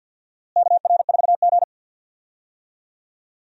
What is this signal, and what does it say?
Morse code, heard as a keyed tone, about 52 wpm, 700 Hz. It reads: KR4G
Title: USA Call Signs - 4 Characters - Repeated 3x - 50wpm